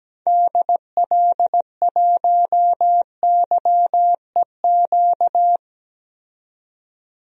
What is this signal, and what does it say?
Morse code, heard as a keyed tone, about 17 wpm, 700 Hz. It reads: DL1YEQ
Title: DXCC-100 Call Signs - 6 Characters - 17wpm